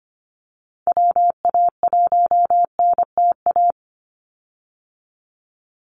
Morse code, heard as a keyed tone, about 25 wpm, 700 Hz. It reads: WA1NTA
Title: Call Signs Repeated 3x - 25wpm